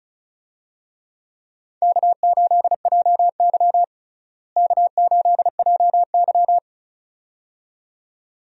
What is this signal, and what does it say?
Morse code, heard as a keyed tone, about 36 wpm, 700 Hz. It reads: K8JY K8JY
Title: USA Call Signs - 4 Characters - Repeated 2x - 35wpm